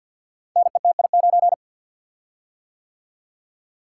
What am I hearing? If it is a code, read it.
Morse code, heard as a keyed tone, about 49 wpm, 700 Hz. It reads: NETI9
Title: Code Groups - 5 Letter:Numbers - 50wpm